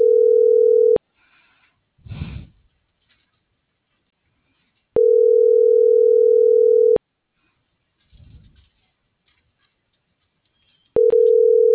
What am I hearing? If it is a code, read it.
Ringback tone (North America), 440 + 480 Hz.